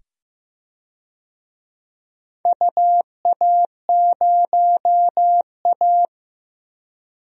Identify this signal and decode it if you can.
Morse code, heard as a keyed tone, about 15 wpm, 700 Hz. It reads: UA0A
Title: DXCC-100 Call Signs - 4 Characters - 15wpm